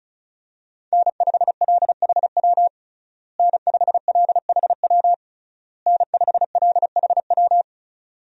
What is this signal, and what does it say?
Morse code, heard as a keyed tone, about 36 wpm, 700 Hz. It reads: N5LHW N5LHW N5LHW
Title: USA Call Signs - 5 Characters - Repeated 3x - 35wpm